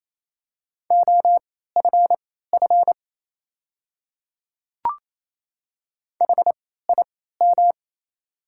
Morse code, heard as a keyed tone, about 27 wpm, spaced as for 16 wpm, 700 Hz. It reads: OFF HIM
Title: 3x Character Spacing - Top 200 Words - 28wpm